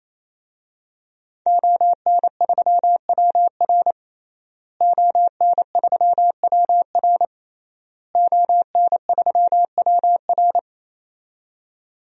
Morse code, heard as a keyed tone, about 27 wpm, 700 Hz. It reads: ON3WR ON3WR ON3WR
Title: DXCC-100 Call Signs - 5 Characters - Repeated 3x - 28wpm